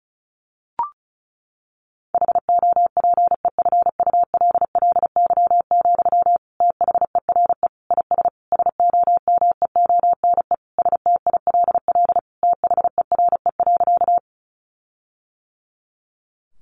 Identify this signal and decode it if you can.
Morse code, heard as a keyed tone, about 34 wpm, 700 Hz. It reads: HOPEFULLY, THERE IS SOMEONE STILL THERE.
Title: Sentences from Top 1500 Words 35wpm